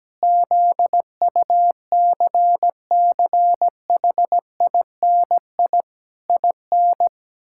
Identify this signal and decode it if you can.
Morse code, heard as a keyed tone, about 17 wpm, 700 Hz. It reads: ZUCCHINI IN
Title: Pangram Challenge v2 - 17wpm